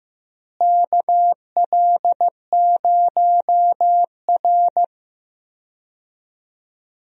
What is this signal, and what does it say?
Morse code, heard as a keyed tone, about 16 wpm, 700 Hz. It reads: KL0R
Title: USA Call Signs - 4 Characters - Repeated 2x - 15wpm